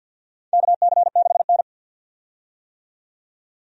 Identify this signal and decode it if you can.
Morse code, heard as a keyed tone, about 52 wpm, 700 Hz. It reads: KC6N
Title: USA Call Signs - 4 Characters - 50wpm